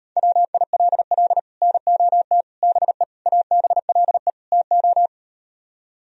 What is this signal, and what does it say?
Morse code, heard as a keyed tone, about 38 wpm, 700 Hz. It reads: WILL NOT BE ABLE TO